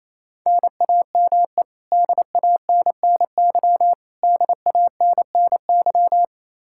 Morse code, heard as a keyed tone, about 28 wpm, 700 Hz. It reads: NAME DANNY DANNY
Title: Practice QSO - Exchange 1 - Beginning - 28wpm